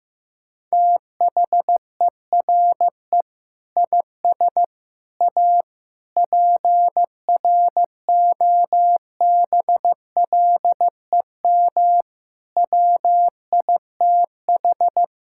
Morse code, heard as a keyed tone, about 16 wpm, 700 Hz. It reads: THERE IS A PROBLEM WITH
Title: Sentences from Top 1000 Words - Review Part 5 - 15wpm